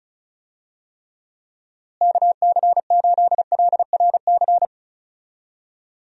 Morse code, heard as a keyed tone, about 34 wpm, 700 Hz. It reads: KC8LRC
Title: USA Call Signs - 6 Characters - Repeated 3x - 35wpm